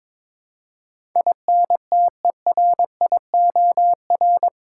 Morse code, heard as a keyed tone, about 22 wpm, 700 Hz. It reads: INTERIOR
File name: practice / 22wpm / Eight Letter Words - 22wpm